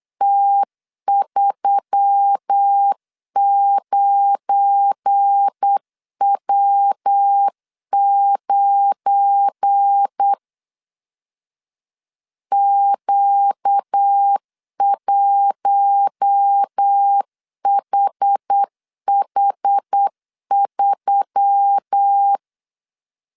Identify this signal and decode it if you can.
Morse code, heard as a keyed tone, about 8 wpm, 800 Hz. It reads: T39W9 Q1HH3
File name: CW / DET / 1 9 1500C S B 8wpm 5624